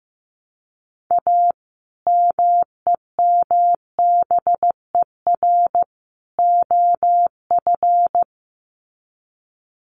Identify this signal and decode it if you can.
Morse code, heard as a keyed tone, about 15 wpm, 700 Hz. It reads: A MEMBER OF
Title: Sets of 3 Words 15wpm